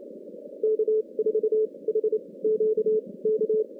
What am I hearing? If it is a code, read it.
Morse code, heard as a keyed tone, about 28 wpm, spaced as for 21 wpm, 410 Hz. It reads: K4HQK
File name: contests / CW-SS-k5tr- -2100